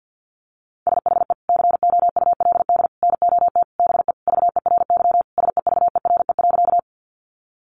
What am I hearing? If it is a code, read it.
Morse code, heard as a keyed tone, about 50 wpm, 700 Hz. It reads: SHE COULD NOT BE VERY SEVERE.
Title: Sentences from Top 1400 Words 50wpm